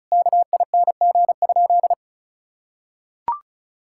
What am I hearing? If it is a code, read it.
Morse code, heard as a keyed tone, about 36 wpm, 700 Hz. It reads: KING?
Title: Sentences from Top 1600 Words 35wpm